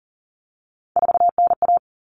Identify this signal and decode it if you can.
Morse code, heard as a keyed tone, about 40 wpm, 700 Hz. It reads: 4NA
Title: Common Cut Numbers - 3-Digit 40wpm